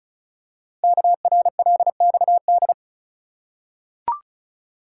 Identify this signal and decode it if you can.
Morse code, heard as a keyed tone, about 36 wpm, 700 Hz. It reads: KRLXD
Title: Code Groups - 5 Letters - 35wpm